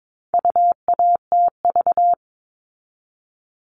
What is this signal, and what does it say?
Morse code, heard as a keyed tone, about 22 wpm, 700 Hz. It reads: UATV
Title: All Cut Numbers - 4-Digit 22wpm